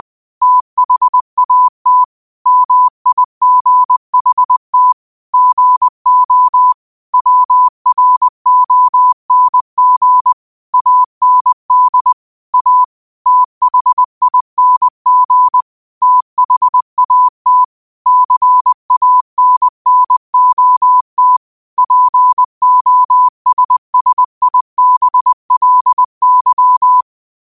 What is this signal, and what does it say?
Morse code, heard as a keyed tone, about 20 wpm, 1000 Hz. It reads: THAT MIGHT GO WRONG AND A THING THAT CANNOT POSSIBLY
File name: qotdfiles / 20wpm